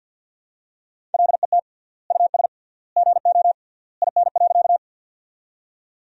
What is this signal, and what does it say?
Morse code, heard as a keyed tone, about 50 wpm, 700 Hz. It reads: LET US GO IN.
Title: 3x Word-Spacing - Sentences from Top 300 Words - 50wpm